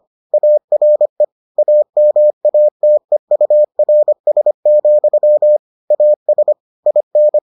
Morse code, heard as a keyed tone, about 25 wpm, 600 Hz. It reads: ARE AMATEURS, AS IN